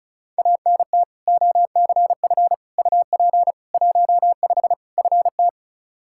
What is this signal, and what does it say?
Morse code, heard as a keyed tone, about 36 wpm, 700 Hz. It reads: ANT OCF UP 15 FT
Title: Practice QSO - Exchange 2 - Info - 35wpm